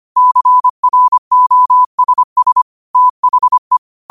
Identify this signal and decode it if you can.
Morse code, heard as a keyed tone, about 25 wpm, 1000 Hz. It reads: CROSS THE
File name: qotdfiles / 25wpm